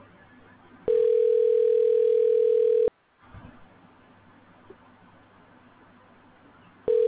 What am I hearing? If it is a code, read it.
Ringback tone (North America), 440 + 480 Hz.